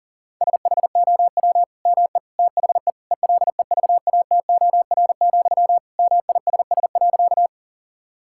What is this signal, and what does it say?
Morse code, heard as a keyed tone, about 40 wpm, 700 Hz. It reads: SHOW ME THE ELEVATOR, MISS.
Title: Sentences from Top 1600 Words 40wpm